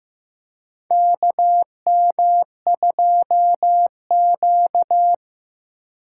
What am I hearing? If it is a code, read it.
Morse code, heard as a keyed tone, about 15 wpm, 700 Hz. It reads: KM2Q
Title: USA Call Signs - 4 Characters - 3x Character Spacing - 15wpm